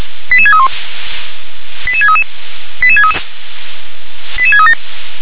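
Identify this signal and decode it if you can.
Five-tone selective call (ZVEI), heard as quick runs of five tones, tones of about 70 ms. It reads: 88631 88633 88633 88637
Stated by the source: ZVEI five-tone selective call received on SDR.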